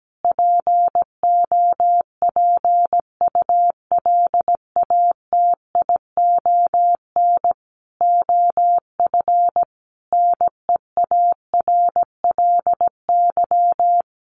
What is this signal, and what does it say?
Morse code, heard as a keyed tone, about 17 wpm, 700 Hz. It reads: POPULATION OF NEARLY
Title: Sentences from Top 1300 Words 17wpm